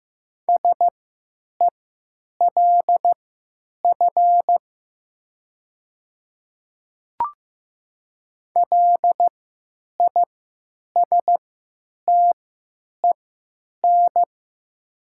Morse code, heard as a keyed tone, about 15 wpm, spaced as for 9 wpm, 700 Hz. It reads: SELF LISTEN
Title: 3x Character Spacing - Top 500 Words - 15wpm